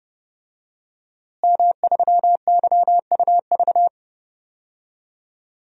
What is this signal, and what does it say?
Morse code, heard as a keyed tone, about 30 wpm, 700 Hz. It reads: M3YUV
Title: DXCC-100 Call Signs - 5 Characters - Repeated 2x - 30wpm